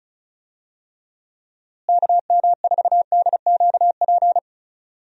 Morse code, heard as a keyed tone, about 34 wpm, 700 Hz. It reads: KM4DQP
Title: USA Call Signs - 6 Characters - Repeated 3x - 35wpm